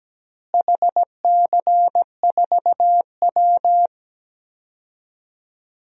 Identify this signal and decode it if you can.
Morse code, heard as a keyed tone, about 17 wpm, 700 Hz. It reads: HC4W
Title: DXCC-100 Call Signs - 4 Characters - Repeated 3x - 17wpm